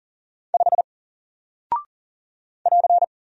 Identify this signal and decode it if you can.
Morse code, heard as a keyed tone, about 40 wpm, 700 Hz. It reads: 5 <AR>
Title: Single Character (Letter:Number:Punctuation:Prosign) Alt 40wpm